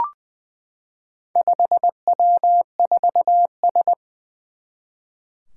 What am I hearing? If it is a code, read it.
Morse code, heard as a keyed tone, about 20 wpm, 700 Hz. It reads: 5W4S
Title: Four Challenging Letter-Number Combinations 20wpm